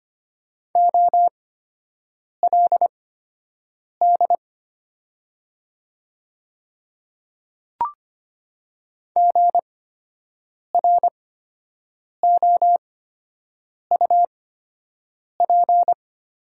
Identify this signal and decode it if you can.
Morse code, heard as a keyed tone, about 25 wpm, spaced as for 7 wpm, 700 Hz. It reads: OLD GROUP
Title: 8x Character Spacing - Top 300 Words - 25wpm